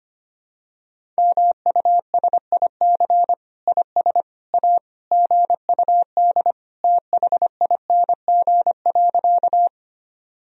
Morse code, heard as a keyed tone, about 25 wpm, 700 Hz. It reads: MUSIC IS A GUD THING.
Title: Sentences from Top 300 Words and Abbreviations 25wpm